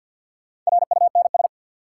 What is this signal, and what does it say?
Morse code, heard as a keyed tone, about 53 wpm, 700 Hz. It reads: RUNS
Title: Sentences from Top 600 Words - New Word Drill - 50wpm